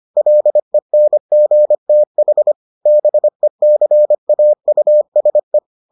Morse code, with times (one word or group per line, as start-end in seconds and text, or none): 0.2-2.5 LENGTH
2.9-5.6 BECAUSE